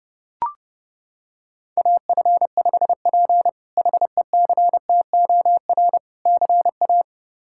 Morse code, heard as a keyed tone, about 32 wpm, 700 Hz. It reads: AF5P HECTOR CA
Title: K1USN SST QSO Practice - 30wpm